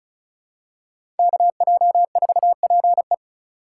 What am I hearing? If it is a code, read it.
Morse code, heard as a keyed tone, about 36 wpm, 700 Hz. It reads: KJ4PE
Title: USA Call Signs - 3x Character Spacing - 35wpm